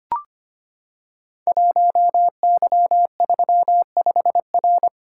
Morse code, heard as a keyed tone, about 25 wpm, 700 Hz. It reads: 1Y35R
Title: Code Groups - 5 Letter:Numbers - 25wpm